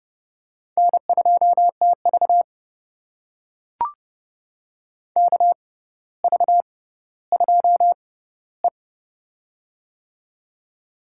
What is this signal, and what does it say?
Morse code, heard as a keyed tone, about 30 wpm, spaced as for 15 wpm, 700 Hz. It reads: N2TV KV2E
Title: USA Call Signs - 4 Characters - 6x Character Spacing - 30wpm